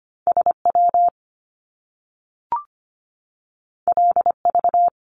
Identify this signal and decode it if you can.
Morse code, heard as a keyed tone, about 25 wpm, 700 Hz. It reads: SW LV
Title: Top Two Letter Combinations in Words 25wpm